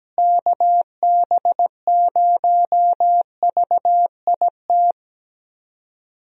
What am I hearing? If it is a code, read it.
Morse code, heard as a keyed tone, about 17 wpm, 700 Hz. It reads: KB0VIT
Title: USA Call Signs - 6 Characters - Repeated 3x - 17wpm